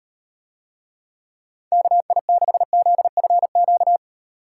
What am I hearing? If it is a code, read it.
Morse code, heard as a keyed tone, about 38 wpm, 700 Hz. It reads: KI6ZFQ